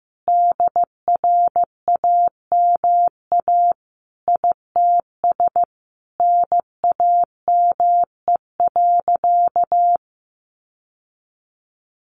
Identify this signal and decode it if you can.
Morse code, heard as a keyed tone, about 15 wpm, 700 Hz. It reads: DRAMA ITS NAME.
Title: Sentences from Top 1500 Words 15wpm